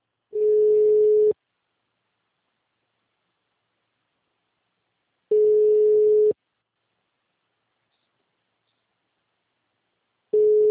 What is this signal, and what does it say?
Ringback tone (Europe, 425 Hz).
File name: out- -2043- - - .18199